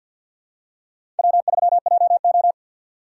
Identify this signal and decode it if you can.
Morse code, heard as a keyed tone, about 52 wpm, 700 Hz. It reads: W3JO
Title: USA Call Signs - 4 Characters - 3x Character Spacing - 50wpm